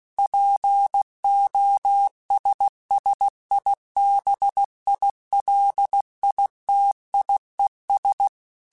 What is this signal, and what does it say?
Morse code, heard as a keyed tone, about 16 wpm, 800 Hz. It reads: POSSIBILITIES
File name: Technical Subjects / Technical Subjects-mp3s / OR Gate @15